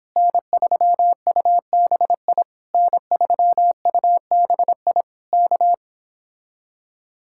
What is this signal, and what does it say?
Morse code, heard as a keyed tone, about 26 wpm, 700 Hz. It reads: N3UBI N3UBI K